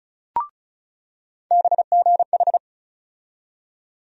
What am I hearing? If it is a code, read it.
Morse code, heard as a keyed tone, about 34 wpm, 700 Hz. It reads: BGH
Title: USA Call Sign Postfixes - 3 Letters 35wpm